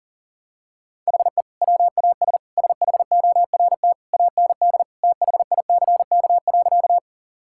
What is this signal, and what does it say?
Morse code, heard as a keyed tone, about 40 wpm, 700 Hz. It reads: HE WAS SHORT AND THICK.
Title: Sentences from Top 1600 Words 40wpm